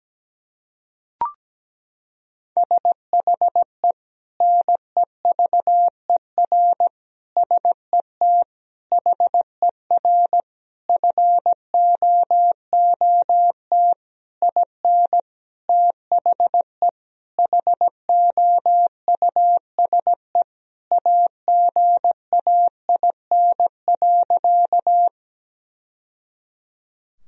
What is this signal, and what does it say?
Morse code, heard as a keyed tone, about 17 wpm, 700 Hz. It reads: SHE NEVER SET HER FOOT IN THE HOUSE AGAIN.
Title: Sentences from Top 1000 Words - Review Part 3 - 17wpm